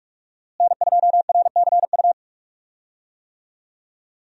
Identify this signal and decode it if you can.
Morse code, heard as a keyed tone, about 46 wpm, 700 Hz. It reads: N2RCU
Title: USA Call Signs - 5 Characters - Repeated 2x - 45wpm